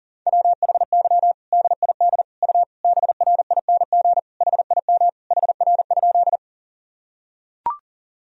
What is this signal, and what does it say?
Morse code, heard as a keyed tone, about 40 wpm, 700 Hz. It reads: WHY DID U BRING HIM HR?
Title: Sentences from Top 500 Words and Abbreviations 40wpm